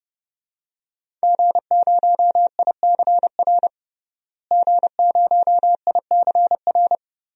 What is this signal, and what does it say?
Morse code, heard as a keyed tone, about 30 wpm, 700 Hz. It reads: G0ICR G0ICR
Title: DXCC-100 Call Signs - 5 Characters - Repeated 3x - 30wpm